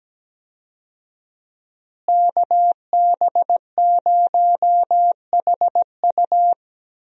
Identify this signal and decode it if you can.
Morse code, heard as a keyed tone, about 17 wpm, 700 Hz. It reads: KB0HU
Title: USA Call Signs - 5 Characters - Repeated 2x - 17wpm